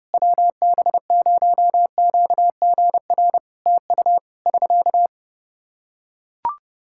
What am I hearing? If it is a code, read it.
Morse code, heard as a keyed tone, about 30 wpm, 700 Hz. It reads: WB0QGR TU <SK>
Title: Practice QSO - Exchange 4 - Ending - 30wpm